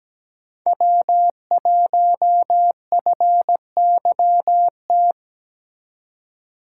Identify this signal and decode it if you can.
Morse code, heard as a keyed tone, about 20 wpm, 700 Hz. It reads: W1FYT
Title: Speed Racing - USA 1x3 Call Signs 17wpm